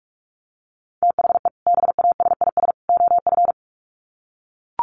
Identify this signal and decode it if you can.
Morse code, heard as a keyed tone, about 44 wpm, 700 Hz. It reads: THE BASIS OF
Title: Sets of 3 Words 45wpm